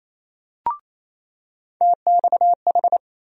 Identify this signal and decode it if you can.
Morse code, heard as a keyed tone, about 28 wpm, 700 Hz. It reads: TXH
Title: USA Call Sign Postfixes - 3 Letters 28wpm